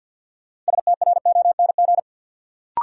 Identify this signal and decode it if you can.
Morse code, heard as a keyed tone, about 50 wpm, 700 Hz. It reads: STRONG
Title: Sentences from Top 500 Words - New Word Drill - 50wpm